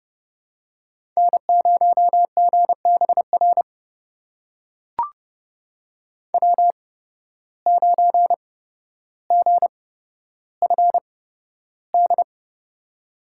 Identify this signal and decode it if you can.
Morse code, heard as a keyed tone, about 32 wpm, spaced as for 13 wpm, 700 Hz. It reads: N0GBR W9GFD
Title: USA Call Signs - 8x Character Spacing - 30wpm